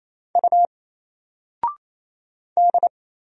Morse code, heard as a keyed tone, about 27 wpm, 700 Hz. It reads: U D
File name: Single-Letter-Number-28wpm